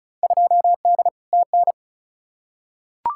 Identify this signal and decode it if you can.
Morse code, heard as a keyed tone, about 36 wpm, 700 Hz. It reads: 2D TN
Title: Field Day Exchanges Encore - 35wpm